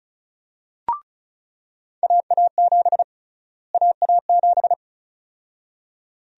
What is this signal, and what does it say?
Morse code, heard as a keyed tone, about 36 wpm, 700 Hz. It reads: AA7 AA7
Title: USA Call Sign Prefixes - Repeated 2x - 35wpm